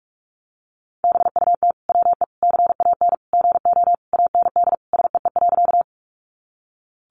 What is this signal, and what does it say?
Morse code, heard as a keyed tone, about 45 wpm, 700 Hz. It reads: BUT WE CAN GO AND SEE.
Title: Sentences from Top 100 Words 45wpm